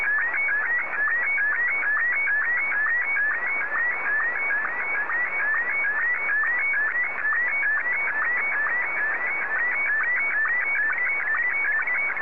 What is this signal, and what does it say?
SSTV (slow-scan television), Martin 1.
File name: sstv / M1-JA3QJQ